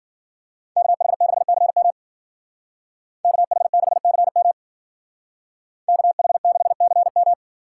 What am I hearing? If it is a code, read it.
Morse code, heard as a keyed tone, about 52 wpm, 700 Hz. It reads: KH6CM KH6CM KH6CM